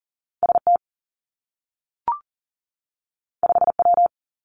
Morse code, heard as a keyed tone, about 41 wpm, 700 Hz. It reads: S T 5 W
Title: Two Challenging Letter-Number Combinations 40wpm